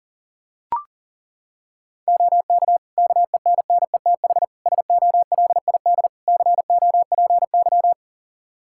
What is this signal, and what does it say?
Morse code, heard as a keyed tone, about 40 wpm, 700 Hz. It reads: OK KENNETH SOLID COPY
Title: Practice QSO - Exchange 3 - More Info - 40wpm